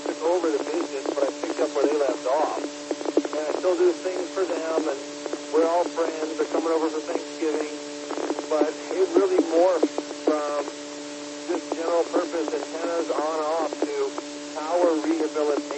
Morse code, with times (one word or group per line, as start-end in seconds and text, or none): none